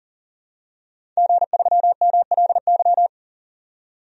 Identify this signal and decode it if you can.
Morse code, heard as a keyed tone, about 40 wpm, 700 Hz. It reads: G3MLY
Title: DXCC-100 Call Signs - 5 Characters - 40wpm